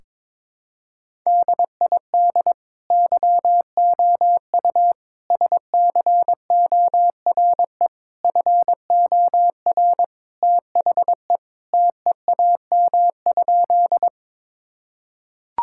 Morse code, heard as a keyed tone, about 22 wpm, 700 Hz. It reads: DID YOU SCORE FOR THE TEAM?
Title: Sentences from Top 1600 Words 22wpm